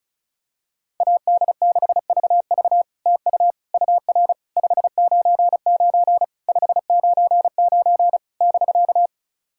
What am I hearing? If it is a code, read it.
Morse code, heard as a keyed tone, about 34 wpm, 700 Hz. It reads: AD6VV TU UR 599 599 <BK>
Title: SOTA QSO Practice 35wpm